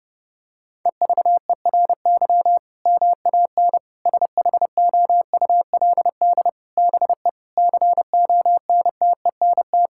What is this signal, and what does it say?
Morse code, heard as a keyed tone, about 32 wpm, 700 Hz. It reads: EVERY MAN SHOULD BE CONTENT
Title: Sentences from Top 1000 Words - Review Part 6 - 30wpm